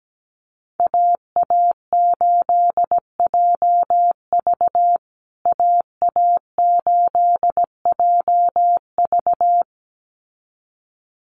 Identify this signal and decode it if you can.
Morse code, heard as a keyed tone, about 17 wpm, 700 Hz. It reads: AA8JV AA8JV
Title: USA Call Signs Repeated 2x 17wpm